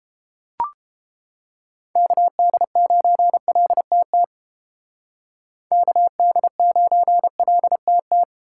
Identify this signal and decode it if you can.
Morse code, heard as a keyed tone, about 32 wpm, 700 Hz. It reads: KD9LTT KD9LTT